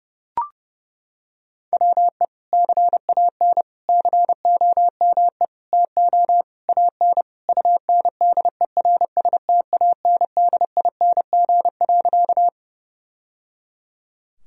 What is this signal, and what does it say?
Morse code, heard as a keyed tone, about 30 wpm, 700 Hz. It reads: WE CAN COME TO AN UNDERSTANDING.
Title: Sentences from Top 1000 Words - Review Part 3 - 30wpm